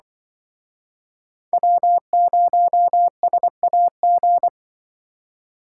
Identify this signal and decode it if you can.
Morse code, heard as a keyed tone, about 23 wpm, 700 Hz. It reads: W0SAG
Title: Speed Racing - USA 1x3 Call Signs 20wpm